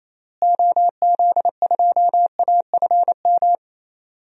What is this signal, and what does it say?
Morse code, heard as a keyed tone, about 28 wpm, 700 Hz. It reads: OZ2AFM
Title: DXCC-100 Call Signs - 6 Characters - Repeated 2x - 28wpm